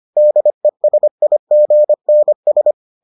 Morse code, heard as a keyed tone, about 25 wpm, 600 Hz. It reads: DESIGNS